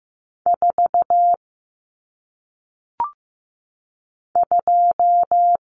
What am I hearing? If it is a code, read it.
Morse code, heard as a keyed tone, about 15 wpm, 700 Hz. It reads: 4 2
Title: Numbers - 1-Digit 15wpm